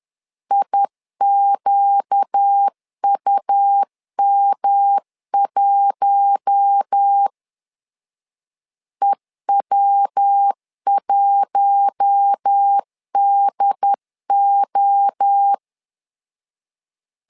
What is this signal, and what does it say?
Morse code, heard as a keyed tone, about 10 wpm, 800 Hz. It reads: IQUM1 EW1DO
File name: LEZIONI / 4 Q1 10wpm 2448